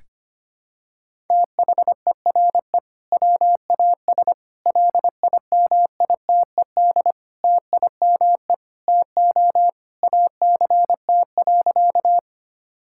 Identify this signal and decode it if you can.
Morse code, heard as a keyed tone, about 25 wpm, 700 Hz. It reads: THERE WAS LIMITED TIME TO ACT.
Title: Sentences from Top 1000 Words - Review Part 6 - 25wpm